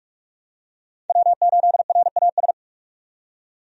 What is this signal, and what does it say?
Morse code, heard as a keyed tone, about 46 wpm, 700 Hz. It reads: W8RAS
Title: USA Call Signs - 5 Characters - 45wpm